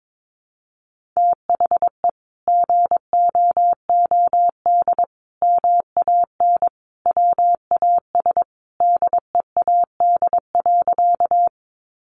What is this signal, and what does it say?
Morse code, heard as a keyed tone, about 22 wpm, 700 Hz. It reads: THE GOOD MAN WAS DEAD.
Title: Sentences from Top 1500 Words 22wpm